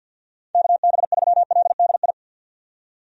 Morse code, heard as a keyed tone, about 49 wpm, 700 Hz. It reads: KB3LDI
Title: USA Call Signs - 6 Characters - 50wpm